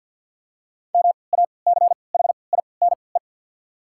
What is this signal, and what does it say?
Morse code, heard as a keyed tone, about 48 wpm, spaced as for 28 wpm, 700 Hz. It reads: MACHINE
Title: 3x Character Spacing - Top 500 Words - 50wpm